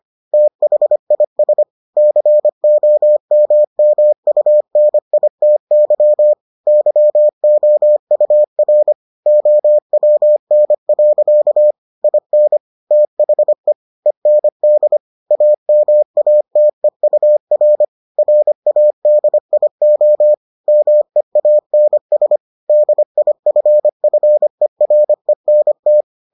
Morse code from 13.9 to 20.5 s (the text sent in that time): END AMATEUR RADIO